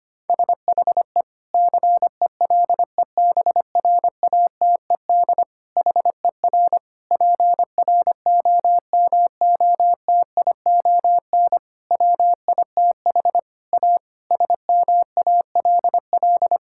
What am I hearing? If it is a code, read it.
Morse code, heard as a keyed tone, about 25 wpm, 700 Hz. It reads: SHE CELEBRATED HER PROMOTION WITH A SMALL
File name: practice / 25wpm / Difficult Sentences with Hints - 25wpm